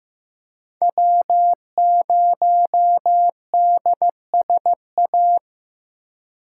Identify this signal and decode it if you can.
Morse code, heard as a keyed tone, about 15 wpm, 700 Hz. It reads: W0DSA
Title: USA Call Signs - 5 Characters - 4x Character Spacing - 15wpm